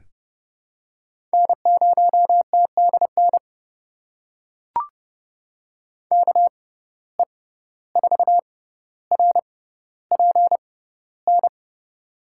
Morse code, heard as a keyed tone, about 31 wpm, spaced as for 15 wpm, 700 Hz. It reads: N0TDN KE4RPN
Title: USA Call Signs - 6x Character Spacing - 30wpm